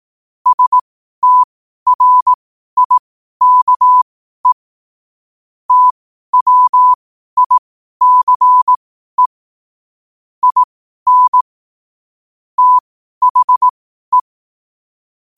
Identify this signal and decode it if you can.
Morse code, heard as a keyed tone, about 18 wpm, spaced as for 12 wpm, 1000 Hz. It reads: STRIKE TWICE IN THE
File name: qotdfiles / 12wpm